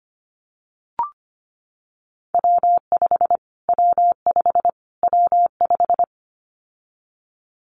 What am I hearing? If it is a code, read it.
Morse code, heard as a keyed tone, about 25 wpm, 700 Hz. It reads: W5 W5 W5
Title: Callsign Prefixes for Top 25 DXCC - Repeated 3x - 25wpm